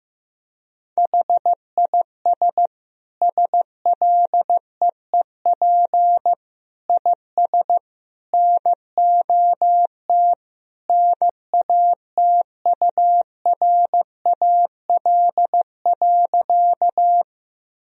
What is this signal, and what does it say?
Morse code, heard as a keyed tone, about 15 wpm, 700 Hz. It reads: HIS SLEEP IS NOT NATURAL.
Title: Sentences from Top 1000 Words - Review Part 4 - 15wpm